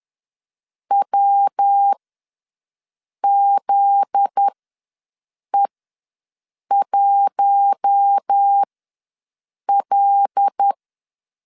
Morse code, heard as a keyed tone, about 10 wpm, spaced as for 6 wpm, 800 Hz. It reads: WZE1L
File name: LEZIONI / 7 GF9Z / 7 GF9Z ws 10wpm 3919